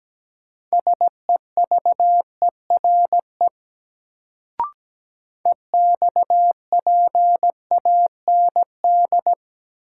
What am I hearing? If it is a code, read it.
Morse code, heard as a keyed tone, about 17 wpm, 700 Hz. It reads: SEVERE EXPAND
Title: Sentences from Top 1400 Words - New Word Drill - 17wpm